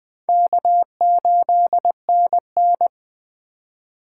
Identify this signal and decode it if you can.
Morse code, heard as a keyed tone, about 24 wpm, 700 Hz. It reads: K8NN
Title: Speed Racing - USA 1x2 Call Signs 20wpm